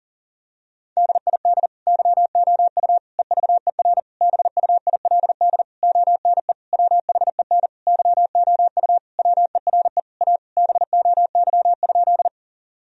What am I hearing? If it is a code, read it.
Morse code, heard as a keyed tone, about 40 wpm, 700 Hz. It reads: DID YOU EVER BUILD ONE WHEN YOU WERE A BOY?
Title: Sentences from Top 1000 Words - Review Part 1 - 40wpm